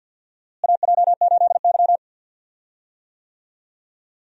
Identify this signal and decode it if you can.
Morse code, heard as a keyed tone, about 50 wpm, 700 Hz. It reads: AJ8Y
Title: USA Call Signs - 4 Characters - Repeated 3x - 50wpm